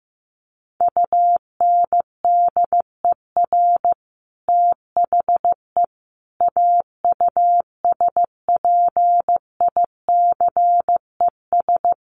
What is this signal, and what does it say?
Morse code, heard as a keyed tone, about 15 wpm, 700 Hz. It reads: UNDER THE AUSPICES
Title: Sets of 5 Words Encore2 15wpm